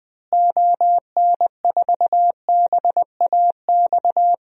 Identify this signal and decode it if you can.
Morse code, heard as a keyed tone, about 20 wpm, 700 Hz. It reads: ON4BAX
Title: DXCC-100 Call Signs - 6 Characters - Repeated 2x - 20wpm